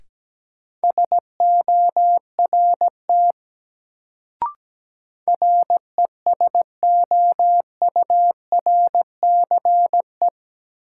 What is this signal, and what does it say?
Morse code, heard as a keyed tone, about 17 wpm, 700 Hz. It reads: SORT RESOURCE
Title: Sentences from Top 1300 Words - New Word Drill Alt - 17wpm